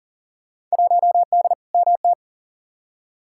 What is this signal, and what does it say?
Morse code, heard as a keyed tone, about 37 wpm, 700 Hz. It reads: 1D MT
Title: Field Day Exchanges - 40wpm